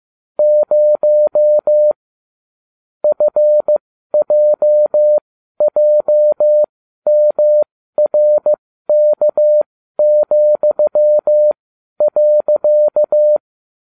Morse code, heard as a keyed tone, about 15 wpm, spaced as for 12 wpm, 600 Hz. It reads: 0 FJJMRK,.